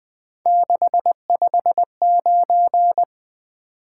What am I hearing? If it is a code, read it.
Morse code, heard as a keyed tone, about 20 wpm, 700 Hz. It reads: 659
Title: Three Letter-Number Combinations - 20wpm